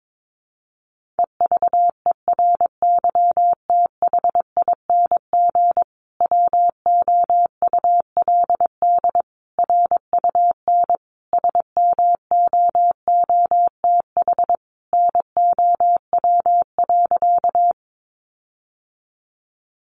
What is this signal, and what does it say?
Morse code, heard as a keyed tone, about 22 wpm, 700 Hz. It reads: EVERYTHING WOULD RUN SMOOTH NOW.
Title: Sentences from Top 1400 Words 22wpm